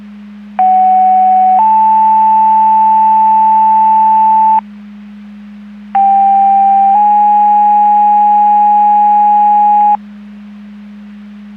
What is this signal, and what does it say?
Two-tone page, heard as a one-second tone, then a three-second tone, about 750 then 880 Hz, about 790 then 830 Hz. It is heard from 0.6 to 10.0 s.
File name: Portage 154.385 Paging